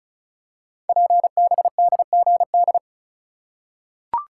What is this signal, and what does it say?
Morse code, heard as a keyed tone, about 36 wpm, 700 Hz. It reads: PBDGD
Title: Code Groups - 5 Letters - 35wpm